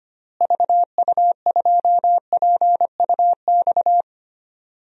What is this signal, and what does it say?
Morse code, heard as a keyed tone, about 25 wpm, 700 Hz. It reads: VU2PUX
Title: DXCC-100 Call Signs - 6 Characters - Repeated 3x - 25wpm